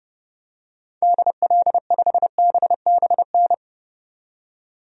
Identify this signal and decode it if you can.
Morse code, heard as a keyed tone, about 30 wpm, 700 Hz. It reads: DL5BBN
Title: DXCC-100 Call Signs - 6 Characters - 30wpm